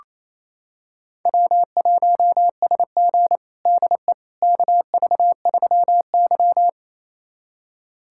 Morse code, heard as a keyed tone, about 28 wpm, 700 Hz. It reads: W1SG DE KV3Y
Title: USA Callsign DE Callsign - 4 Characters - 28wpm